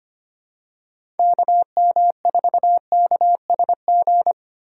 Morse code, heard as a keyed tone, about 25 wpm, 700 Hz. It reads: KM4KSG
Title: USA Call Signs - 6 Characters - 2x Character Spacing - 25wpm